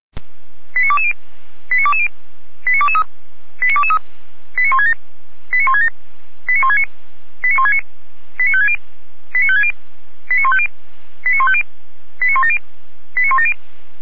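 Five-tone selective call (ZVEI), heard as quick runs of five tones, tones of about 70 ms. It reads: 89220 89220 89223 89223 89167 89167 89169 89169 89560 89560 89150 89150 89170 89170
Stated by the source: ZVEI five-tone selective call received on SDR, from the Feuerwehr Orlamünde audio server.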